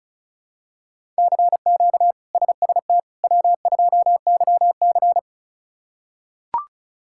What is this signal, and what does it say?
Morse code, heard as a keyed tone, about 34 wpm, 700 Hz. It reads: CQ SST W2YC
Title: K1USN SST QSO Practice - 35wpm